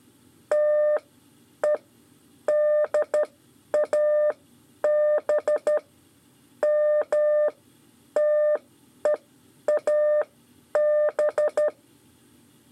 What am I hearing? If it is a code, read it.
Morse code, heard as a keyed tone, about 12 wpm, spaced as for 9 wpm, 600 Hz. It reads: TEDAB MTEAB
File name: 5 LEZIONE RIPASSO 2